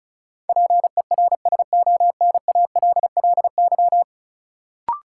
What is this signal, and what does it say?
Morse code, heard as a keyed tone, about 36 wpm, 700 Hz. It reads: PERSONALLY
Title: Sentences from Top 1400 Words - New Word Drill - 35wpm